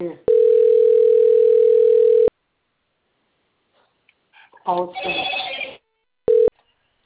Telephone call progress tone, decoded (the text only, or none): ringback tone